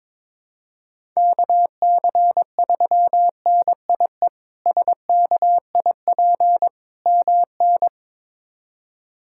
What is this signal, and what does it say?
Morse code, heard as a keyed tone, about 22 wpm, 700 Hz. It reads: KC3NIE SKIP MN
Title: K1USN SST QSO Practice - 22wpm